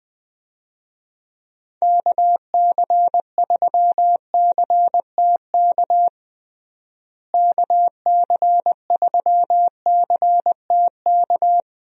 Morse code, heard as a keyed tone, about 20 wpm, 700 Hz. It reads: KC3CTK KC3CTK
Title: USA Call Signs - 6 Characters - Repeated 3x - 20wpm